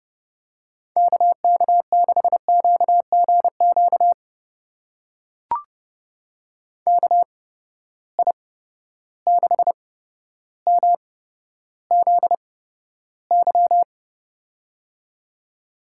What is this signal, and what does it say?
Morse code, heard as a keyed tone, about 30 wpm, spaced as for 13 wpm, 700 Hz. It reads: KK6QGQ KI6MZY
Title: USA Call Signs - 6 Characters - 8x Character Spacing - 30wpm